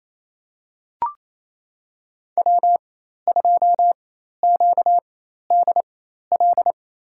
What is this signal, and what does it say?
Morse code, heard as a keyed tone, about 28 wpm, spaced as for 13 wpm, 700 Hz. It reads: W2QDL
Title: USA Call Signs - 5 Characters - 4x Character Spacing - 28wpm